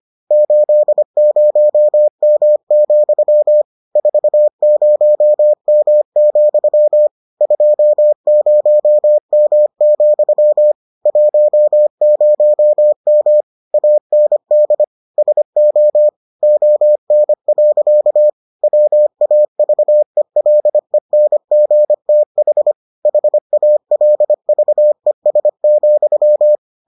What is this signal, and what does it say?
Morse code, heard as a keyed tone, about 25 wpm, 600 Hz. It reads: 80M, 40M, 20M, 10M AND SO ON. WAVELENGTH HALVES,